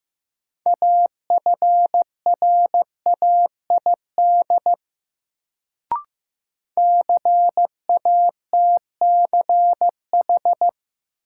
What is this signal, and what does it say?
Morse code, heard as a keyed tone, about 15 wpm, 700 Hz. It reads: AFRAID CATCH
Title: Sentences from Top 1500 Words - New Word Drill Alt - 15wpm